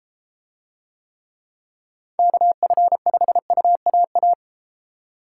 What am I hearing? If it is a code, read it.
Morse code, heard as a keyed tone, about 33 wpm, 700 Hz. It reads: KF5UAA